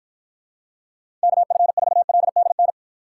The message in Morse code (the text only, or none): KF4LDN